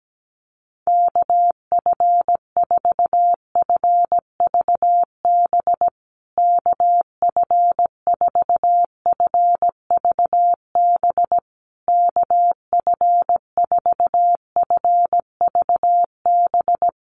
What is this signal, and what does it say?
Morse code, heard as a keyed tone, about 17 wpm, 700 Hz. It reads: KF4FVB KF4FVB KF4FVB
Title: USA Call Signs Repeated 3x 17wpm